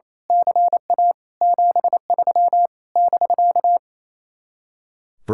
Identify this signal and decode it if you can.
Morse code, heard as a keyed tone, about 27 wpm, 700 Hz. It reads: CA 73 <BK>
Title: Speed-Racing Common QSO Phrases Encore - 28wpm